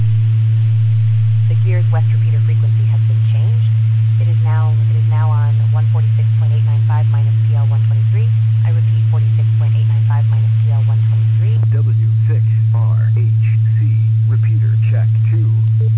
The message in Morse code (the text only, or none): none